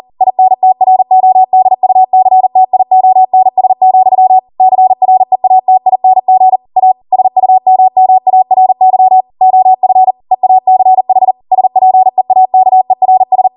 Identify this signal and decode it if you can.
Morse code, heard as a keyed tone, about 40 wpm, 750 Hz. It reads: INTRODUCTIONS, CREATING A SUMMARY OF EACH SPEAKERS